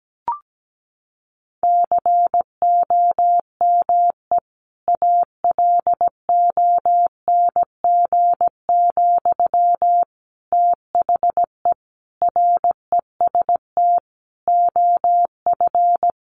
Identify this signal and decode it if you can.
Morse code, heard as a keyed tone, about 17 wpm, 700 Hz. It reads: COME ALONG, THE REST OF
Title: Sentences from Top 700 Words 17wpm